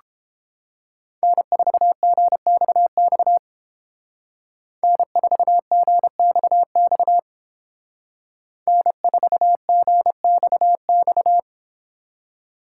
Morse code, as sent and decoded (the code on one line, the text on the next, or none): -. ....- --. -..- -..- / -. ....- --. -..- -..- / -. ....- --. -..- -..-
N4GXX N4GXX N4GXX